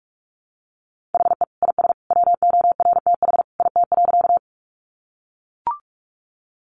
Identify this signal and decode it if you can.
Morse code, heard as a keyed tone, about 44 wpm, 700 Hz. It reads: HE IS WORTH IT.
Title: Sentences from Top 800 Words 45wpm